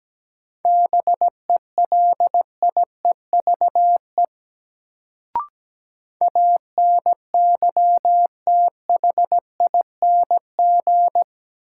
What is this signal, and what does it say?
Morse code, heard as a keyed tone, about 17 wpm, 700 Hz. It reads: BELIEVE ANYTHING
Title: Sentences from Top 1000 Words - New Word Drill - 17wpm